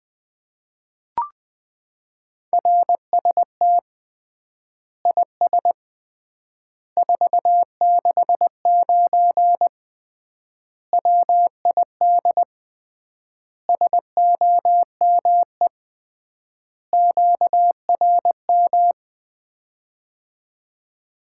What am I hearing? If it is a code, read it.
Morse code, heard as a keyed tone, about 20 wpm, 700 Hz. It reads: RST IS 469 WID SOME QRM
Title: Common QSO Phrases - 3x Word-Spacing - 20wpm